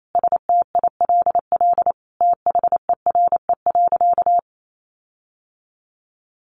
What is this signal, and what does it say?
Morse code, heard as a keyed tone, about 28 wpm, 700 Hz. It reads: STILL THERE.
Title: Sentences from Top 1500 Words 28wpm